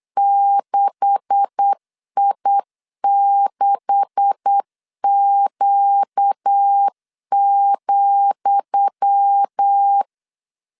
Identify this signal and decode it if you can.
Morse code, heard as a keyed tone, about 8 wpm, 800 Hz. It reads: 6I6Q,
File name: LEZIONI / 10 SIGNS / 10D COMA 8wpm 3749